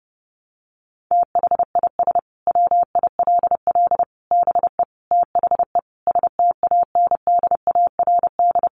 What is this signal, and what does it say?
Morse code, heard as a keyed tone, about 30 wpm, 700 Hz. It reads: THIS WILL BE THE STANDARD
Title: Sentences from Top 1200 Words 30wpm